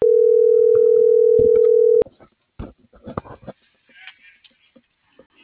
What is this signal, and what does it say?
Ringback tone (North America), 440 + 480 Hz.